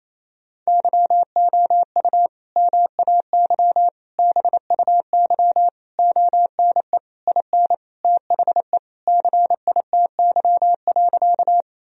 Morse code, heard as a keyed tone, about 28 wpm, 700 Hz. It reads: YOU MAY BUY ONE IN THE CITY.
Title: Sentences from Top 1000 Words - Review Part 4 - 28wpm